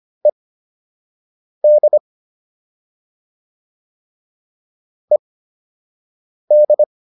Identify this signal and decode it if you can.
Morse code, heard as a keyed tone, about 25 wpm, spaced as for 6 wpm, 600 Hz. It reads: ED ED